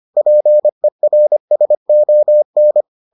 Morse code, heard as a keyed tone, about 25 wpm, 600 Hz. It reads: PERSON